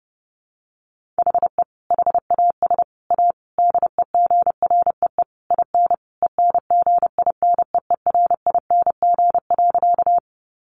Morse code, heard as a keyed tone, about 28 wpm, 700 Hz. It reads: HE HAS A DEGREE IN ENGINEERING.
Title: Sentences from Top 1500 Words 30wpm